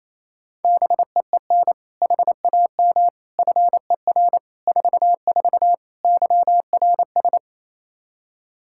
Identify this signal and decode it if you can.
Morse code, heard as a keyed tone, about 28 wpm, 700 Hz. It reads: BEEN HAM FER 44 YRS
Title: Practice QSO - Exchange 3 - More Info - 28wpm